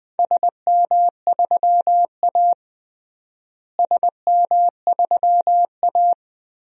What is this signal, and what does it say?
Morse code, heard as a keyed tone, about 20 wpm, 700 Hz. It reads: SM3A SM3A
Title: DXCC-100 Call Signs - 4 Characters - Repeated 2x - 20wpm